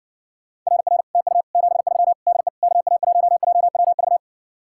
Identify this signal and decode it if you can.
Morse code, heard as a keyed tone, about 60 wpm, 700 Hz. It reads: RR TU 73 DE KA1JWV